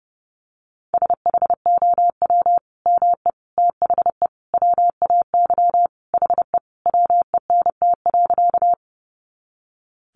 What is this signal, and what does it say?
Morse code, heard as a keyed tone, about 28 wpm, 700 Hz. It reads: SHOW ME THE WAY HE WENT.
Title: Sentences from Top 200 Words 30wpm